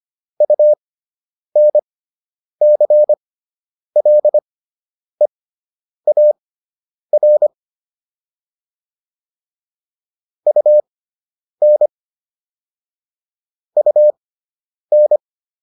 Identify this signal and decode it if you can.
Morse code, heard as a keyed tone, about 25 wpm, spaced as for 9 wpm, 600 Hz. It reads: UNCLEAR UN UN